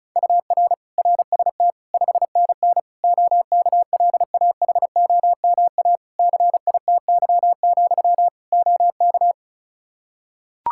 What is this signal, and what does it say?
Morse code, heard as a keyed tone, about 36 wpm, 700 Hz. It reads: UR RST 5NN OKLAHOMA CITY, OK
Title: Practice QSO - Exchange 1 - Beginning - 35wpm